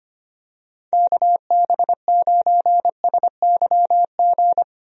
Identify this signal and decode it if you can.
Morse code, heard as a keyed tone, about 25 wpm, 700 Hz. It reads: KB9SYG
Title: USA Call Signs - 6 Characters - 25wpm